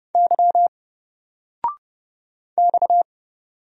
Morse code, heard as a keyed tone, about 30 wpm, 700 Hz. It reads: Y X
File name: Single-Letter-Number-30wpm